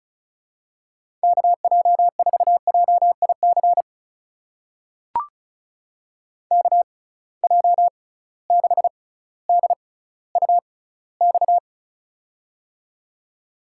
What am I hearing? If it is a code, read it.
Morse code, heard as a keyed tone, about 36 wpm, spaced as for 18 wpm, 700 Hz. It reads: KJ4JIC KJ6DUX
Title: USA Call Signs - 6 Characters - 6x Character Spacing - 35wpm